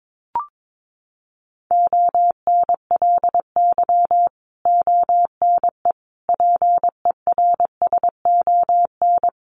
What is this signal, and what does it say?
Morse code, heard as a keyed tone, about 22 wpm, 700 Hz. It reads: ONLY ONE PERSON
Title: Sentences from Top 1100 Words 22wpm